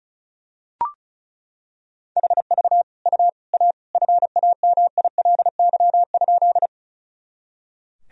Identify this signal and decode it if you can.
Morse code, heard as a keyed tone, about 34 wpm, 700 Hz. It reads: HV U A FAMILY?
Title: Sentences from Top 400 Words and Abbreviations 35wpm